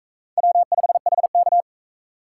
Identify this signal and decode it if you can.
Morse code, heard as a keyed tone, about 36 wpm, 700 Hz. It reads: W5HK